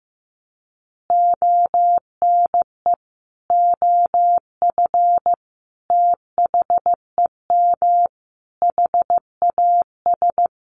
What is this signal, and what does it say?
Morse code, heard as a keyed tone, about 15 wpm, 700 Hz. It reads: ONE OF THEM HAS
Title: Additional Easy Phrases 15wpm